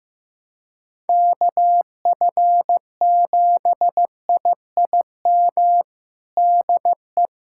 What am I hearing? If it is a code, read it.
Morse code, heard as a keyed tone, about 15 wpm, 700 Hz. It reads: KF7IIM DE
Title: USA Callsign DE Callsign - 6 Characters - 15wpm